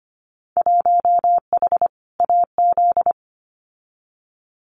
Morse code, heard as a keyed tone, about 25 wpm, 700 Hz. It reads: 1H AZ
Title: Winter Field Day Exchanges 25wpm